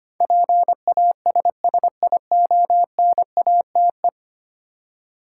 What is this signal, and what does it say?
Morse code, heard as a keyed tone, about 25 wpm, 700 Hz. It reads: PASSIONATE
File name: 10 Letter Words - First 3 Letters Given - 25wpm